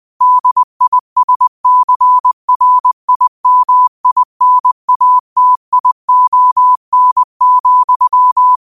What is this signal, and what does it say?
Morse code, heard as a keyed tone, about 20 wpm, 1000 Hz. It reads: DISCRIMINATION,